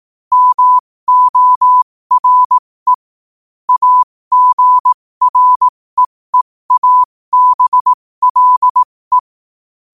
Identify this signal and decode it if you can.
Morse code, heard as a keyed tone, about 18 wpm, spaced as for 15 wpm, 1000 Hz. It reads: MORE AGREEABLE
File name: qotdfiles / 15wpm